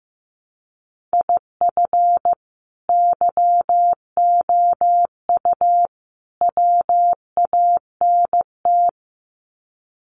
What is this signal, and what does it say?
Morse code, heard as a keyed tone, about 15 wpm, 700 Hz. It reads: IF YOU WANT
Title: Sets of 3 Words 15wpm